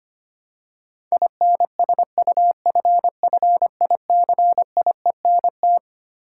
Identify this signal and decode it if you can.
Morse code, heard as a keyed tone, about 25 wpm, 700 Hz. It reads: INSUFFICIENT
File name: Twelve Letter Words - 25wpm